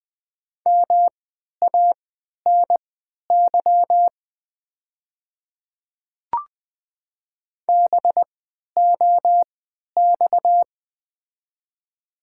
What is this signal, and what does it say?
Morse code, heard as a keyed tone, about 20 wpm, spaced as for 11 wpm, 700 Hz. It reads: MANY BOX
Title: 3x Character Spacing - Top 500 Words - 20wpm